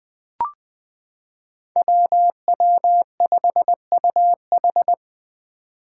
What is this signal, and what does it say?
Morse code, heard as a keyed tone, about 21 wpm, 700 Hz. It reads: WW5UH
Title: Five Challenging Letter-Number Combinations 20wpm